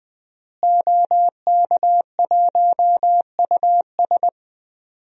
Morse code, heard as a keyed tone, about 21 wpm, 700 Hz. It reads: OK1US
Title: DXCC-100 Call Signs - 5 Characters - Repeated 3x - 20wpm